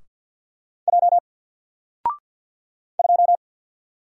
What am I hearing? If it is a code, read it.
Morse code, heard as a keyed tone, about 48 wpm, 700 Hz. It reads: J 2
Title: Single Letter-Number Alt 50wpm